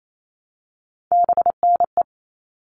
Morse code, heard as a keyed tone, about 28 wpm, 700 Hz. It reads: BNE
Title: All Cut Numbers - 3-Digit 28wpm